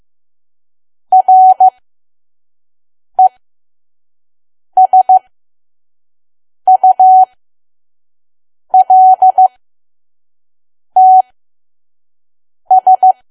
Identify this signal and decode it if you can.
Morse code, heard as a keyed tone, about 15 wpm, spaced as for 5 wpm, 750 Hz. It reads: RESULTS
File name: Archive / 5 WPM files / PM